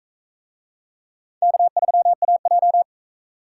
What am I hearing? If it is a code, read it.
Morse code, heard as a keyed tone, about 44 wpm, 700 Hz. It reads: K3AJ